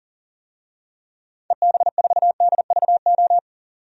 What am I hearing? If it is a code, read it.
Morse code, heard as a keyed tone, about 40 wpm, 700 Hz. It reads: EB4DVO
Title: DXCC-100 Call Signs - 6 Characters - 40wpm